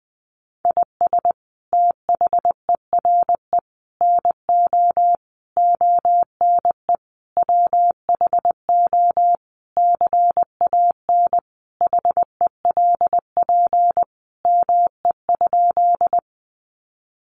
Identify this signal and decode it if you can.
Morse code, heard as a keyed tone, about 20 wpm, 700 Hz. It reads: IS THERE NO ONE WHO CAN HELP ME?
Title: Sentences from Top 200 Words 20wpm